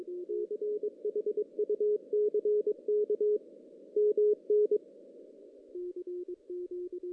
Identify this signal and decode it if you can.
Morse code, heard as a keyed tone, about 23 wpm, 420 Hz. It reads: CHUCK MN
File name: 2016 JAN CW NAQP / 2016 Jan naqp k5ot - 0-5